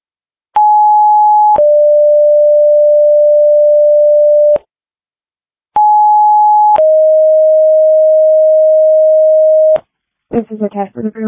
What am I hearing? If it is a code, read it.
Two-tone page, heard as a one-second tone, then a three-second tone, about 840 then 590 Hz, about 840 then 620 Hz, with speech. It is heard from 0.6 to 9.8 s.
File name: dispatch / Denver Fire 2026 07 16 19 00 19